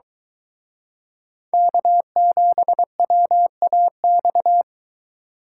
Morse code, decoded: K7WAX

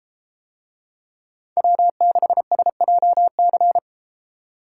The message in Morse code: W6SJC

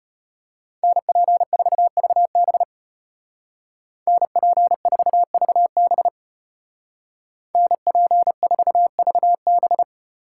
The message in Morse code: NP4VB NP4VB NP4VB

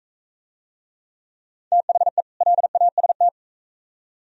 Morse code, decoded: THE LAST

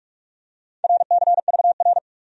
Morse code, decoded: RCVR

